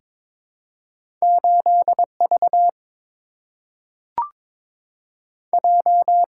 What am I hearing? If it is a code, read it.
Morse code, heard as a keyed tone, about 22 wpm, 700 Hz. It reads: 8V J
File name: practice / 22wpm / Two Letter-Number Combinations - 22wpm